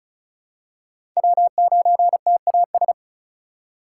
Morse code, heard as a keyed tone, about 36 wpm, 700 Hz. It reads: W9TAS
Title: USA Call Signs - 5 Characters - 2x Character Spacing - 35wpm